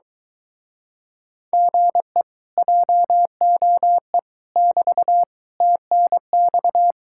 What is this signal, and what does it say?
Morse code, heard as a keyed tone, about 23 wpm, 700 Hz. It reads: GE JOE <BT> TNX